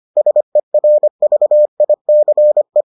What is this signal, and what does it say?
Morse code, heard as a keyed tone, about 25 wpm, 600 Hz. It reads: SERVICE